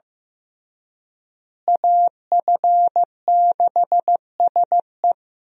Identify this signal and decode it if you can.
Morse code, heard as a keyed tone, about 15 wpm, 700 Hz. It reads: AF6SE
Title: USA Call Signs - 5 Characters - Repeated 3x - 15wpm